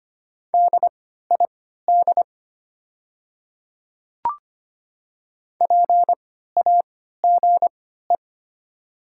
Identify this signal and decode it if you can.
Morse code, heard as a keyed tone, about 25 wpm, spaced as for 14 wpm, 700 Hz. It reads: DID PAGE
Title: 3x Character Spacing - Top 500 Words - 25wpm